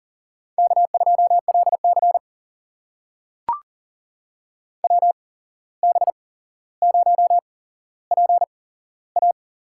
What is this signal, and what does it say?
Morse code, heard as a keyed tone, about 40 wpm, spaced as for 16 wpm, 700 Hz. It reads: K2LC WB0PA
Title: USA Call Signs - 8x Character Spacing - 40wpm